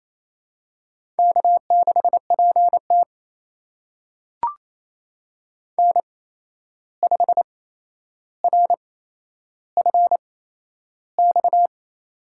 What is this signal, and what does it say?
Morse code, heard as a keyed tone, about 28 wpm, spaced as for 11 wpm, 700 Hz. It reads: K6PT N5RFX
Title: USA Call Signs - 8x Character Spacing - 28wpm